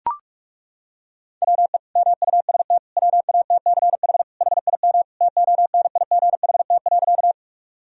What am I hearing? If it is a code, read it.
Morse code, heard as a keyed tone, about 41 wpm, 700 Hz. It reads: WE MUST WATCH HIM TONIGHT.
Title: Sentences from Top 1000 Words - Review Part 3 - 45wpm